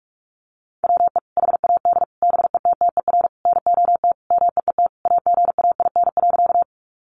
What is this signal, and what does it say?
Morse code, heard as a keyed tone, about 44 wpm, 700 Hz. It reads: WE HAD BETTER NOT MEET AGAIN.
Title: Sentences from Top 1100 Words 45wpm